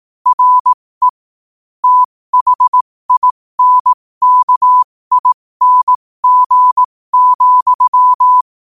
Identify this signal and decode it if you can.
Morse code, heard as a keyed tone, about 18 wpm, spaced as for 15 wpm, 1000 Hz. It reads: RE THINKING,
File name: qotdfiles / 15wpm